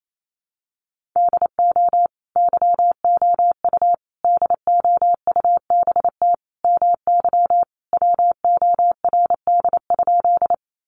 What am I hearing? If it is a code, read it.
Morse code, heard as a keyed tone, about 28 wpm, 700 Hz. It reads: DO YOU DOUBT MY WORD?
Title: Sentences from Top 1500 Words 28wpm